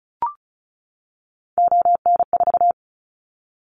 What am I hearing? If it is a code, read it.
Morse code, heard as a keyed tone, about 36 wpm, 700 Hz. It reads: ON4
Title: Callsign Prefixes for Top 50 DXCC - 35wpm